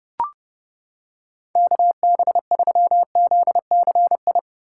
Morse code, heard as a keyed tone, about 28 wpm, 700 Hz. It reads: KB3ZCI